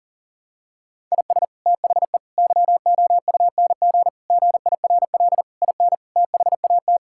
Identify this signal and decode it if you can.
Morse code, heard as a keyed tone, about 37 wpm, 700 Hz. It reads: IS THE YOUNG GIRL IN THAT
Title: Sentences from Top 1000 Words - Review Part 5 - 40wpm